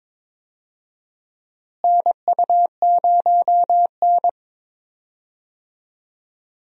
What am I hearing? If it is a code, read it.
Morse code, heard as a keyed tone, about 22 wpm, 700 Hz. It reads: NU0N
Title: USA Call Signs - 4 Characters - Repeated 2x - 22wpm